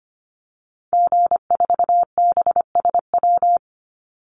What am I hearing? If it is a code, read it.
Morse code, heard as a keyed tone, about 25 wpm, 700 Hz. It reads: G4BSW
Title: Call Signs Repeated 3x - 25wpm